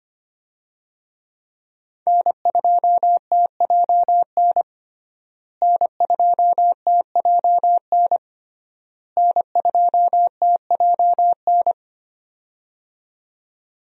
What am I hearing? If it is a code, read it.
Morse code, heard as a keyed tone, about 25 wpm, 700 Hz. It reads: N2TJN N2TJN N2TJN
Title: USA Call Signs - 5 Characters - Repeated 3x - 25wpm